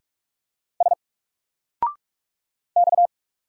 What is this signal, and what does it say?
Morse code, heard as a keyed tone, about 44 wpm, 700 Hz. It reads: S X
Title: Single Character (Letter:Number:Punctuation:Prosign) Alt 45wpm